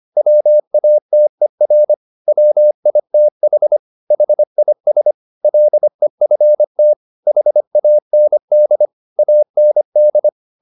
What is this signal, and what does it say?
Morse code, heard as a keyed tone, about 25 wpm, 600 Hz. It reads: WATER WITH HIS LEFT HAND AND